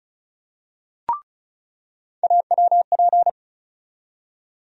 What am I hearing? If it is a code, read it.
Morse code, heard as a keyed tone, about 36 wpm, 700 Hz. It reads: AWP
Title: USA Call Sign Postfixes - 3 Letters 35wpm